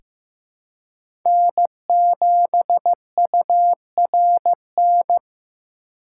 Morse code, heard as a keyed tone, about 15 wpm, 700 Hz. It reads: N7URN
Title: USA Call Signs - 5 Characters - 6x Character Spacing - 15wpm